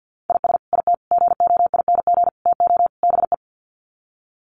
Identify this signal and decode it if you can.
Morse code, heard as a keyed tone, about 50 wpm, 700 Hz. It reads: IS IT GOING TO BE
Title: Sets of 5 Words Encore2 50wpm